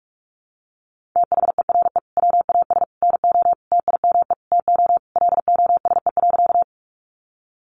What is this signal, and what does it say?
Morse code, heard as a keyed tone, about 44 wpm, 700 Hz. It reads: THERE WAS NO TIME TO LOSE.
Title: Sentences from Top 1100 Words 45wpm